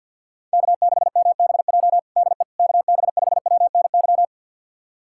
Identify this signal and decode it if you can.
Morse code, heard as a keyed tone, about 52 wpm, 700 Hz. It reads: K6MBJ DE KB5WNY
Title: USA Callsign DE Callsign 50wpm